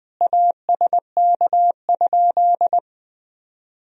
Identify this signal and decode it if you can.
Morse code, heard as a keyed tone, about 20 wpm, 700 Hz. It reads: ASK?
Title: Sentences from Top 200 Words and Abbreviations 20wpm